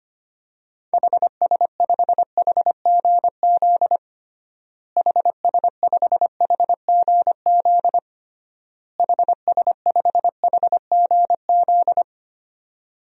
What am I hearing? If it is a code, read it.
Morse code, heard as a keyed tone, about 25 wpm, 700 Hz. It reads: HS5HGZ HS5HGZ HS5HGZ
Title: DXCC-100 Call Signs - 6 Characters - Repeated 3x - 25wpm